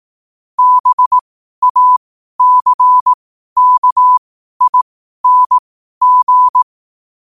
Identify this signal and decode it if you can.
Morse code, heard as a keyed tone, about 18 wpm, spaced as for 12 wpm, 1000 Hz. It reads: BACKING
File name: qotdfiles / 12wpm